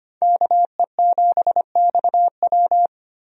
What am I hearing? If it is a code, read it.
Morse code, heard as a keyed tone, about 25 wpm, 700 Hz. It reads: KE7XW
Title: USA Call Signs - 5 Characters - Repeated 2x - 25wpm